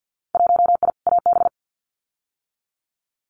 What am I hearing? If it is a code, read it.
Morse code, heard as a keyed tone, about 50 wpm, 700 Hz. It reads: 1I AB
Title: Winter Field Day Exchanges Encore 50wpm